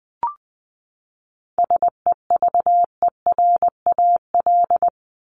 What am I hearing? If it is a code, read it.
Morse code, heard as a keyed tone, about 20 wpm, 700 Hz. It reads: SEVERAL
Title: Top 500 Words 20wpm